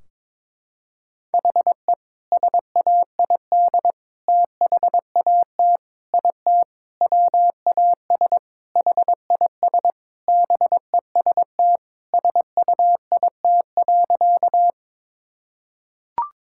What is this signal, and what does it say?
Morse code, heard as a keyed tone, about 22 wpm, 700 Hz. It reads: HE SAID THAT IT WAS HIS BEST SUIT.
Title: Sentences from Top 1600 Words 22wpm